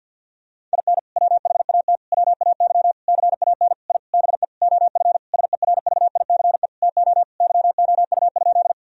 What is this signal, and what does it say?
Morse code, heard as a keyed tone, about 50 wpm, 700 Hz. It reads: IN WHAT WAY CAN I BE OF SERVICE TO YOU?
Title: Sentences from Top 1000 Words - Review Part 4 - 50wpm